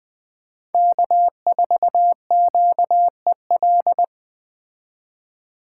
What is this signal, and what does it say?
Morse code, heard as a keyed tone, about 20 wpm, 700 Hz. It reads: K4QEL
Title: Speed Racing - USA Call Signs 17wpm